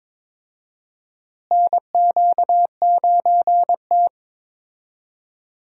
Morse code, heard as a keyed tone, about 22 wpm, 700 Hz. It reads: NQ9T
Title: USA Call Signs - 4 Characters - Repeated 2x - 22wpm